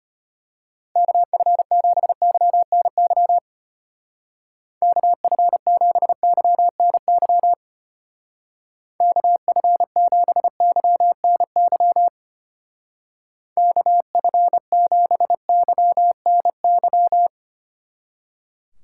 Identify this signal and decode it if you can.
Morse code, heard as a keyed tone, about 31 wpm, 700 Hz. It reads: KF7YNY KF7YNY KF7YNY KF7YNY
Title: Speed Racing - USA 2x3 Call Signs 25wpm